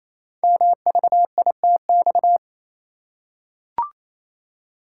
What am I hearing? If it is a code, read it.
Morse code, heard as a keyed tone, about 27 wpm, 700 Hz. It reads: MVITX
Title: Code Groups - 5 Letters - 28wpm